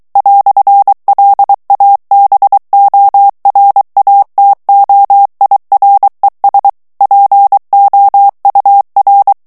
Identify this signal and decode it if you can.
Morse code, heard as a keyed tone, about 23 wpm, 800 Hz. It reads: "LABORATOIRES POUR
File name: morse mp3 / FRANCAIS TEXTE CLAIR 1 VITESSE 22 MOTS MINUTES